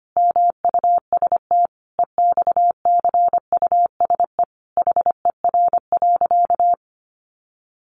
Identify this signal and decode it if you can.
Morse code, heard as a keyed tone, about 25 wpm, 700 Hz. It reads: MUST EXCUSE HER.
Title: Sentences from Top 700 Words 25wpm